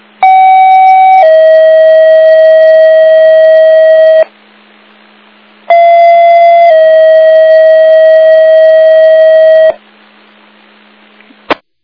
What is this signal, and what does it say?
Two-tone page, heard as a one-second tone, then a three-second tone, about 760 then 640 Hz, about 680 then 640 Hz. It is heard from 0.2 to 9.7 s.